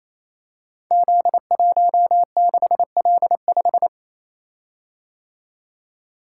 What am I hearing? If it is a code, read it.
Morse code, heard as a keyed tone, about 28 wpm, 700 Hz. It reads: Z16L5
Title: Code Groups - 5 Letter:Numbers - 28wpm